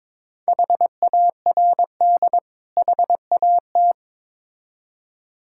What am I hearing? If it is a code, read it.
Morse code, heard as a keyed tone, about 23 wpm, 690 Hz. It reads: HARD HAT
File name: Difficult Words with Hints - 22wpm